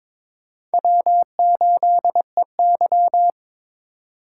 Morse code, heard as a keyed tone, about 22 wpm, 700 Hz. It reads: W8EY
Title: USA Call Signs - 4 Characters - 3x Character Spacing - 22wpm